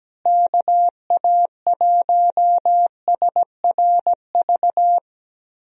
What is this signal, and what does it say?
Morse code, heard as a keyed tone, about 17 wpm, 700 Hz. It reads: KA1SRV
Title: USA Call Signs - 6 Characters - Repeated 3x - 17wpm